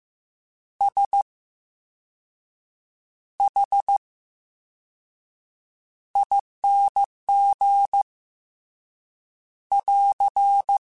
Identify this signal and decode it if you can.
Morse code, heard as a keyed tone, about 15 wpm, spaced as for 6 wpm, 800 Hz. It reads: SHING <AR>
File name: Special INGs-mp3s / Special INGs-mp3s / VerySlowToFaster Ing @13